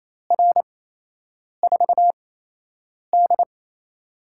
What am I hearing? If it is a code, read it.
Morse code, heard as a keyed tone, about 27 wpm, spaced as for 8 wpm, 700 Hz. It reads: R4D
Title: USA Call Signs - 4 Characters - 8x Character Spacing - 28wpm